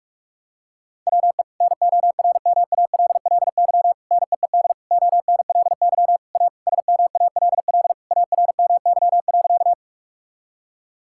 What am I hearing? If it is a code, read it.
Morse code, heard as a keyed tone, about 46 wpm, 700 Hz. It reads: WE NORMALLY NEED ONLY A SMALL ARMY.
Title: Sentences from Top 1000 Words - Review Part 6 - 45wpm